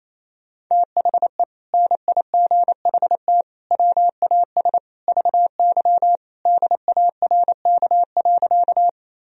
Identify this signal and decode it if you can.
Morse code, heard as a keyed tone, about 28 wpm, 700 Hz. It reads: THE NIGHT WAS VY DARK.
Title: Sentences from Top 500 Words and Abbreviations 28wpm